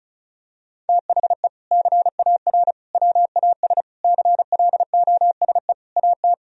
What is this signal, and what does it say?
Morse code, heard as a keyed tone, about 36 wpm, 700 Hz. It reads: THE CAR WAS CLOSE AT
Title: Sentences from Top 1000 Words - Review Part 1 - 35wpm